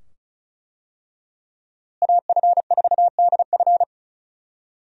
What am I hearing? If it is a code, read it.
Morse code, heard as a keyed tone, about 36 wpm, 700 Hz. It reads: AF4DF
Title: USA Call Signs - 5 Characters - Repeated 2x - 35wpm